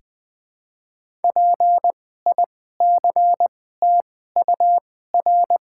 Morse code, heard as a keyed tone, about 20 wpm, spaced as for 14 wpm, 700 Hz. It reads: PICTUR
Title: 2x Character Spacing - Top 200 Words - 20wpm